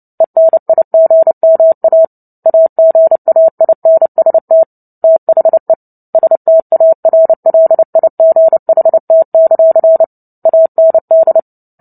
Morse code, heard as a keyed tone, about 30 wpm, 650 Hz. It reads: ENIGMA AGAINST THE STARLIGHT; AND
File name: Book Two / 03-Chapter 3 - 29 WPM